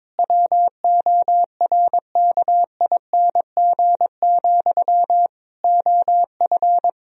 Morse code, heard as a keyed tone, about 22 wpm, 700 Hz. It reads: WORKING, OF
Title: Sentences from Top 1000 Words - Review Part 4 - 22wpm